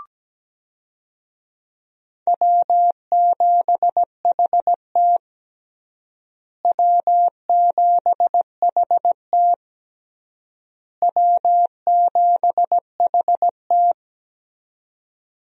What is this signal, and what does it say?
Morse code, heard as a keyed tone, about 17 wpm, 700 Hz. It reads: W7HT W7HT W7HT
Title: USA Call Signs - 4 Characters - Repeated 3x - 17wpm